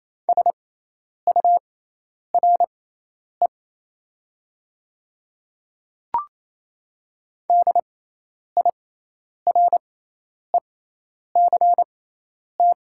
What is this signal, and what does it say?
Morse code, heard as a keyed tone, about 27 wpm, spaced as for 10 wpm, 700 Hz. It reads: SURE DIRECT
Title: 6x Character Spacing - Top 500 Words - 28wpm